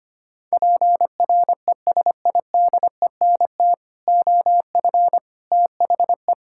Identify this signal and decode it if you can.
Morse code, heard as a keyed tone, about 25 wpm, 700 Hz. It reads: PRESIDENT OF THE
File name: practice / 25wpm / Sentences with 4-Word Phrases - 25wpm